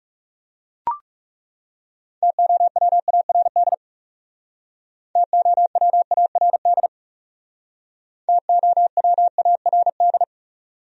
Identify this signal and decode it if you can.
Morse code, heard as a keyed tone, about 36 wpm, 700 Hz. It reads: TOWARD TOWARD TOWARD
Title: Speed-Racing Top 500 Words 30wpm